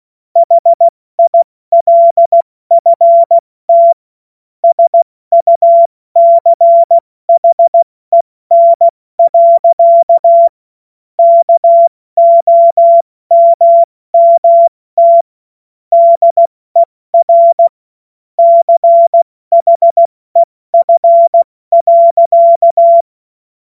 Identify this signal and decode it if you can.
Morse code, heard as a keyed tone, about 16 wpm, 680 Hz. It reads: HILFT SUCHEN. KOMMT DER CHEF.
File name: Morsekurs / Witze / Witz 20. Januar 2026 70 F80